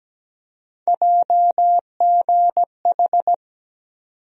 Morse code, heard as a keyed tone, about 17 wpm, 700 Hz. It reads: JGH
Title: USA Call Sign Postfixes - 3 Letters 17wpm